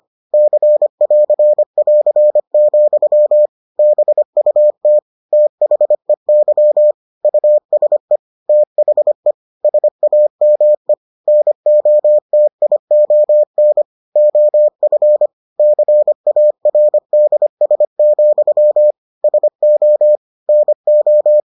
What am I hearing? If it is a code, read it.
Morse code, heard as a keyed tone, about 25 wpm, 600 Hz. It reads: C<AR><AR>, BUT THEY USE THE SAME NOTION OF CARDS, SO NO